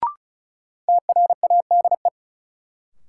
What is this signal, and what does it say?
Morse code, heard as a keyed tone, about 36 wpm, 700 Hz. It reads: TRADE
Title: Sets of 5 Words Encore - New Word Drill - 35wpm